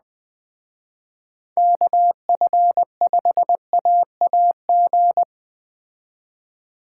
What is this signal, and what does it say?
Morse code, heard as a keyed tone, about 20 wpm, 700 Hz. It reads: KF5AAG